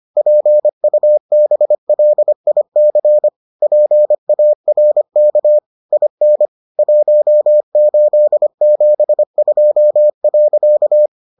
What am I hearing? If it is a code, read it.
Morse code, heard as a keyed tone, about 25 wpm, 600 Hz. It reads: PUBLIC PARK IN 1872.